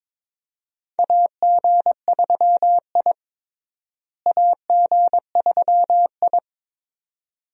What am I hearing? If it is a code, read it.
Morse code, heard as a keyed tone, about 22 wpm, 700 Hz. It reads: AG3I AG3I
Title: USA Call Signs - 4 Characters - Repeated 2x - 22wpm